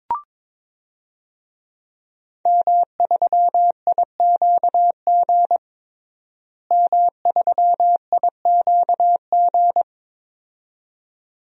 Morse code, heard as a keyed tone, about 22 wpm, 700 Hz. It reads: M3IQG M3IQG
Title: DXCC-100 Call Signs - 5 Characters - Repeated 2x - 22wpm